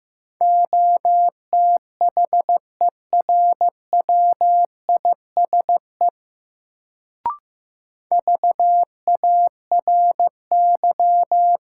Morse code, heard as a keyed tone, about 15 wpm, 700 Hz. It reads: OTHERWISE VARY
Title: Sentences from Top 1200 Words - New Word Drill - 15wpm